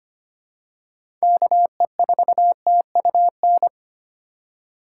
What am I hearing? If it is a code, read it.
Morse code, heard as a keyed tone, about 25 wpm, 700 Hz. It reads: KE4TUN
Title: USA Call Signs - 6 Characters - 3x Character Spacing - 25wpm